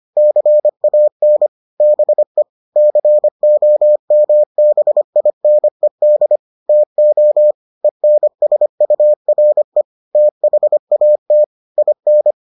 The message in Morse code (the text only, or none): CAN BE COMBINED TO ENSURE THAT IN